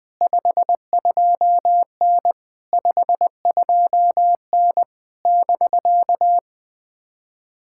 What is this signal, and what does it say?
Morse code, heard as a keyed tone, about 19 wpm, 700 Hz. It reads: 52N 52N <BK>
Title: POTA QSO Practice - 20wpm